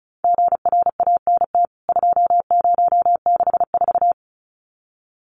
Morse code, heard as a keyed tone, about 34 wpm, 700 Hz. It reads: GRANT 2064
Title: CWT Exchange 35wpm